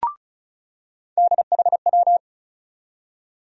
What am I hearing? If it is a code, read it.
Morse code, heard as a keyed tone, about 34 wpm, 700 Hz. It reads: DHW
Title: USA Call Sign Postfixes - 3 Letters 35wpm